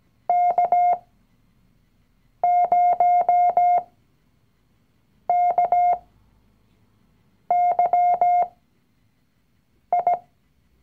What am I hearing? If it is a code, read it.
Morse code, heard as a keyed tone, about 17 wpm, spaced as for 5 wpm, 710 Hz. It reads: K0KYI
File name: CW-Online-Kurs / Anfänger / Lektion 019